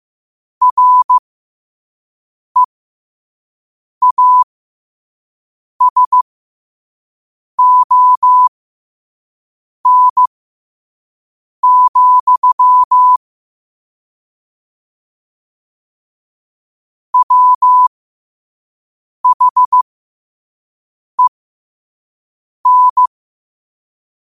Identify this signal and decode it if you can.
Morse code, heard as a keyed tone, about 15 wpm, spaced as for 5 wpm, 1000 Hz. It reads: REASON, WHEN
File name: qotdfiles / 5wpm